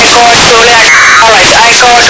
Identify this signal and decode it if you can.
Packet radio (AX.25, as in APRS), with speech.
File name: xw1 fm afsk